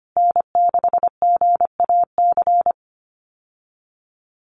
Morse code, heard as a keyed tone, about 25 wpm, 700 Hz. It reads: N6GAC
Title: USA Call Signs Repeated 3x 25wpm